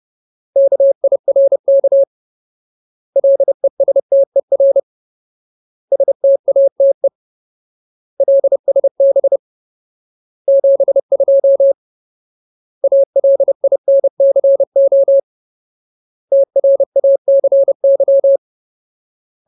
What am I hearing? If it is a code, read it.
Morse code, heard as a keyed tone, about 30 wpm, 550 Hz. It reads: KIRK LESTER STATE LSB 72 ALINCO TRACY
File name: Ultimate Word List / 30WPM Ultimate